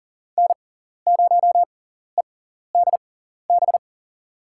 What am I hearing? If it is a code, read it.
Morse code, heard as a keyed tone, about 40 wpm, spaced as for 14 wpm, 700 Hz. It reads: N0EDB
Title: USA Call Signs - 5 Characters - 6x Character Spacing - 40wpm